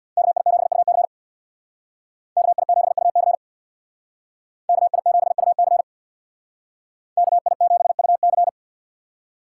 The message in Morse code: KI7UC KI7UC KI7UC KI7UC

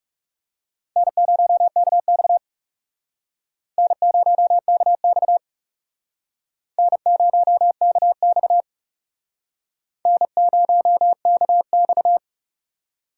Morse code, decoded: N0KX N0KX N0KX N0KX

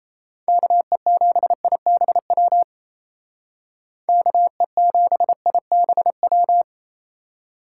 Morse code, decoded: KE7IBW KE7IBW

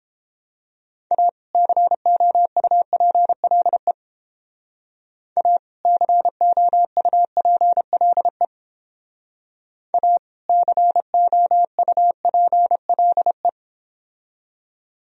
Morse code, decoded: A COUPLE A COUPLE A COUPLE

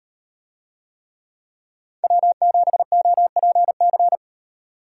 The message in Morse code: W7OPC

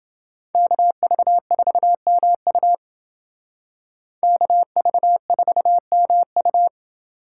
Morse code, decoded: KV4MU KV4MU

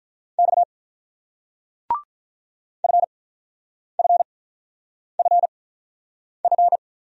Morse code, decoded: X FFFF

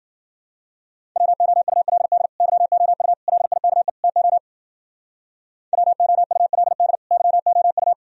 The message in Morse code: WOULD YOU LIKE TO WOULD YOU

